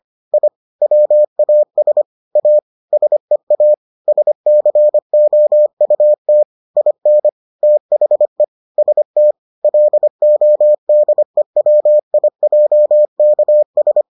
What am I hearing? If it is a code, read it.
Morse code, heard as a keyed tone, about 25 wpm, 600 Hz. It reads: I WAS A SEA SCOUT IN THE ST LODEWIJKS